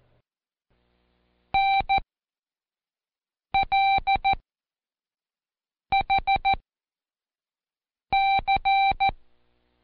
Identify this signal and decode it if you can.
Morse code, heard as a keyed tone, about 14 wpm, spaced as for 5 wpm, 780 Hz. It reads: NLHC
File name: Code Quick / AudioDownld1 / Seg36-Day4-Intro N